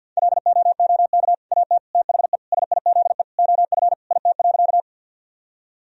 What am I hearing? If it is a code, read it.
Morse code, heard as a keyed tone, about 50 wpm, 700 Hz. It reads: LOOK AT THE SIZE OF IT.
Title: Sentences from Top 1000 Words - Review Part 2 - 50wpm